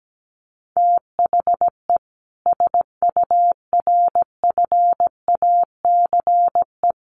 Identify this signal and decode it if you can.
Morse code, heard as a keyed tone, about 17 wpm, 700 Hz. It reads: THE SURFACE
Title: Sets of 4 Words 17wpm